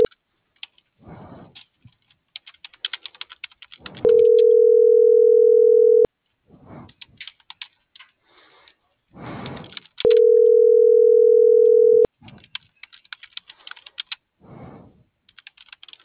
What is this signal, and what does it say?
Ringback tone (North America), 440 + 480 Hz.